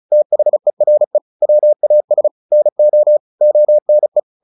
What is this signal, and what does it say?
Morse code, heard as a keyed tone, about 34 wpm, 600 Hz. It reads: THERE WAS NO ONE
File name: the old man and the sea 2 35wpm